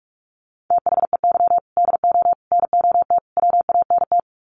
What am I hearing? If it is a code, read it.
Morse code, heard as a keyed tone, about 44 wpm, 700 Hz. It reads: THEY DO NOT WANT
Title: Sets of 4 Words 45wpm